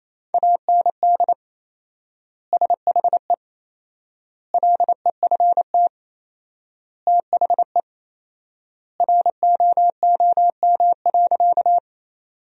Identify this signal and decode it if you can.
Morse code, heard as a keyed tone, about 28 wpm, 700 Hz. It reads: AND SHE LEFT THE ROOM.
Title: 4x Word-Spacing - Sentences from Top 300 Words - 28wpm